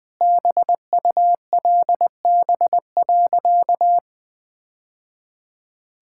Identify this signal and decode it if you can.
Morse code, heard as a keyed tone, about 20 wpm, 700 Hz. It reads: BULB.
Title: Easy Sentences with Hints - 20wpm